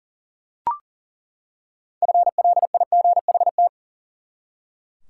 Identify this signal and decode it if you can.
Morse code, heard as a keyed tone, about 40 wpm, 700 Hz. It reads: FLIGHT
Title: Six Letter Words - 40wpm